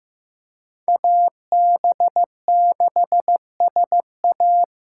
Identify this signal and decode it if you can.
Morse code, heard as a keyed tone, about 15 wpm, 700 Hz. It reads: AB6SA
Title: USA Call Signs - 5 Characters - 3x Character Spacing - 15wpm